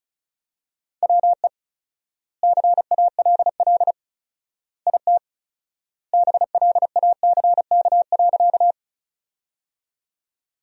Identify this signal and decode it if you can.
Morse code, heard as a keyed tone, about 34 wpm, 700 Hz. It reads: WE CALL IT BLACK.
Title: 4x Word-Spacing - Sentences from Top 400 Words - 35wpm